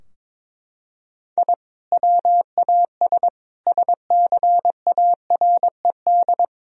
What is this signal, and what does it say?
Morse code, heard as a keyed tone, about 21 wpm, 700 Hz. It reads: I WAS SCARED
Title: Sentences from Top 1000 Words - Review Part 4 - 22wpm